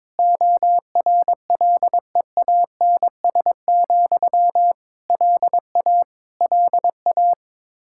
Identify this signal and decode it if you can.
Morse code, heard as a keyed tone, about 22 wpm, 700 Hz. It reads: ORLEANS, LA LA
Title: Practice QSO - Exchange 1 - Beginning - 22wpm